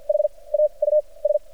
Morse code, heard as a keyed tone, about 47 wpm, spaced as for 27 wpm, 600 Hz. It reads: HAUS
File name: archived / CRSnet